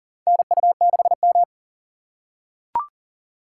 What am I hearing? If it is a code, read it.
Morse code, heard as a keyed tone, about 39 wpm, 700 Hz. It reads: NU6M
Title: USA Call Signs - 4 Characters - 2x Character Spacing - 40wpm